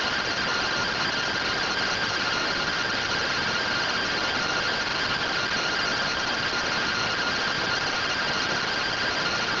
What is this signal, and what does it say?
Two-tone teleprinter signal, about 170 Hz shift.
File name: findings / MR AXIAL fr1 120.gen-12 rtty rx